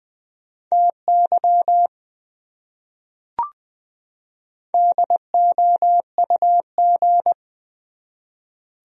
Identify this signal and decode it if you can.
Morse code, heard as a keyed tone, about 21 wpm, 700 Hz. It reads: TY DOUG
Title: USA Common Names - 20wpm